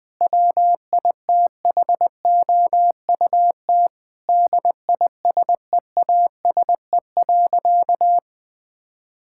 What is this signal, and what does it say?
Morse code, heard as a keyed tone, about 21 wpm, 700 Hz. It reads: WITHOUT DISEASE.
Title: Sentences from Top 1000 Words - Review Part 6 - 20wpm